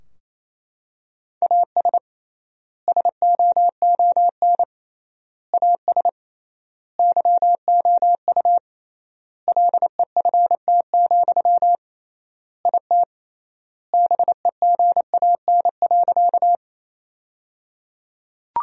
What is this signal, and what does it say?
Morse code, heard as a keyed tone, about 28 wpm, 700 Hz. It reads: AS SOON AS YOU LEFT, IT BEGAN.
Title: 3x Word-Spacing - Sentences from Top 400 Words - 28wpm